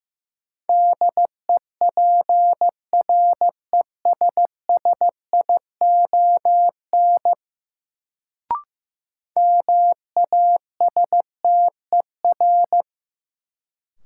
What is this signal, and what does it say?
Morse code, heard as a keyed tone, about 16 wpm, 700 Hz. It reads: DEPRESSION MASTER
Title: Sentences from Top 1300 Words - New Word Drill Alt - 15wpm